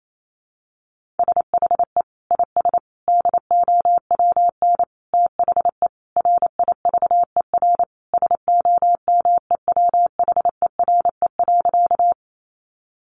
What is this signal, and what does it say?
Morse code, heard as a keyed tone, about 28 wpm, 700 Hz. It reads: SHE IS DOWN THE RIVER SOMEWHERE.
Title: Sentences from Top 1400 Words 28wpm